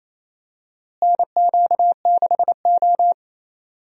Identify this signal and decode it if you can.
Morse code, heard as a keyed tone, about 28 wpm, 700 Hz. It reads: NQ6O
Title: USA Call Signs - 4 Characters - Repeated 3x - 28wpm